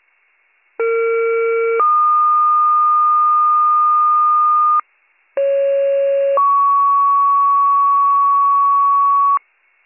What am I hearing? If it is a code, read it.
Two-tone page, heard as a one-second tone, then a three-second tone, about 460 then 1220 Hz, about 550 then 1120 Hz. It is heard from 0.8 to 9.4 s.